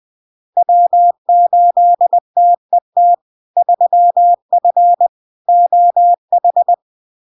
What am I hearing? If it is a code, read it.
Morse code, heard as a keyed tone, about 20 wpm, 700 Hz. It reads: W8TET 3F OH
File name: LARK-FD-2022-Log-in-CW-20wpm-700-Hz